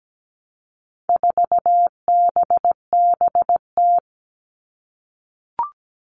Morse code, heard as a keyed tone, about 17 wpm, 700 Hz. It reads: 4BBT
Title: All Cut Numbers - 4-Digit 17wpm